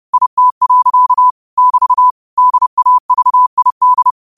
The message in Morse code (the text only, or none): IT. <BT> DAVID